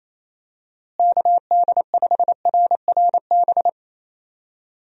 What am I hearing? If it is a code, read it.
Morse code, heard as a keyed tone, about 28 wpm, 700 Hz. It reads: KD5RRB
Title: USA Call Signs - 6 Characters - Repeated 2x - 28wpm